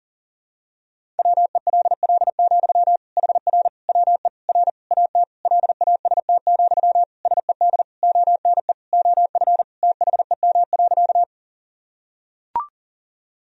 Morse code, heard as a keyed tone, about 37 wpm, 700 Hz. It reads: WELL, HR WE R AT LAST, SED ONE OF THEM.
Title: Sentences from Top 300 Words and Abbreviations 40wpm